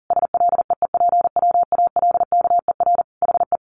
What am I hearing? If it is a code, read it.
Morse code, heard as a keyed tone, about 40 wpm, 700 Hz. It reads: SLEEPWALKER HE
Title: A Sound Of Thunder 40wpm